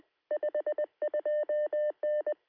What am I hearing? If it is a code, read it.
Morse code, heard as a keyed tone, about 20 wpm, 600 Hz. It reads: 52N
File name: sm7lcb / 2116z oh3lwp 23cm